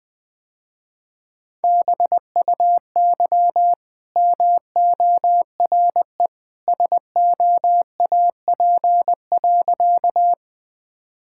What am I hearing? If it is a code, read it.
Morse code, heard as a keyed tone, about 20 wpm, 700 Hz. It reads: BUY MORE SOAP.
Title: Easy Sentences with Hints - 20wpm